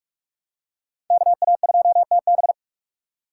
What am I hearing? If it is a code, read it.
Morse code, heard as a keyed tone, about 44 wpm, 700 Hz. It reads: KA2TB
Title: Code Groups - 5 Letter:Numbers - 45wpm